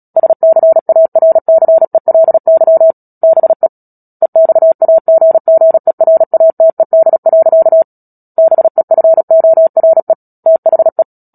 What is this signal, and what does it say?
Morse code, heard as a keyed tone, about 37 wpm, 650 Hz. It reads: SCARCELY BE EXAGGERATED. BEFORE THE
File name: Book Two / 10-Chapter 10 - Epilogue - 36 WPM